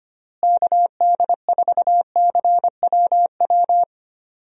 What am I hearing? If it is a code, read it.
Morse code, heard as a keyed tone, about 25 wpm, 700 Hz. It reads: KD4CWW
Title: USA Call Signs - 6 Characters - Repeated 3x - 25wpm